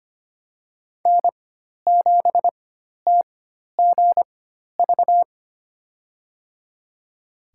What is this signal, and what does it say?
Morse code, heard as a keyed tone, about 25 wpm, spaced as for 12 wpm, 700 Hz. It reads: N7TGV
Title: USA Call Signs - 5 Characters - 4x Character Spacing - 25wpm